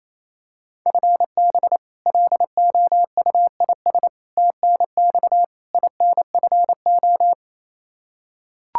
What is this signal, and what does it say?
Morse code, heard as a keyed tone, about 28 wpm, 700 Hz. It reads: FB LOUIS TNX INFO
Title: Practice QSO - Exchange 3 - More Info - 28wpm